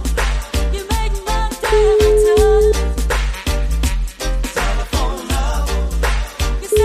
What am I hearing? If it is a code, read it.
Ringback tone (Europe, 425 Hz).